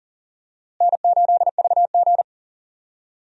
Morse code, heard as a keyed tone, about 40 wpm, 700 Hz. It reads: N8VG
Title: USA Call Signs - 4 Characters - Repeated 2x - 40wpm